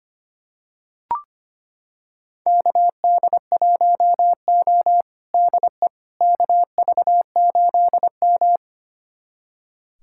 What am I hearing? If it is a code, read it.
Morse code, heard as a keyed tone, about 25 wpm, 700 Hz. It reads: KD1O DE KV8M
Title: USA Callsign DE Callsign - 4 Characters - 25wpm